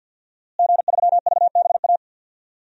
Morse code, heard as a keyed tone, about 49 wpm, 700 Hz. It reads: G3VBA
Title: DXCC-100 Call Signs - 5 Characters - Repeated 3x - 50wpm